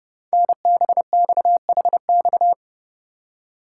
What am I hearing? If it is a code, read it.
Morse code, heard as a keyed tone, about 28 wpm, 700 Hz. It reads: NBXHX
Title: Code Groups - 5 Letters - 30wpm